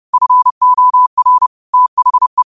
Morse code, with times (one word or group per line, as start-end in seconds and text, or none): 0.1-1.5 FOR
1.7-2.4 THE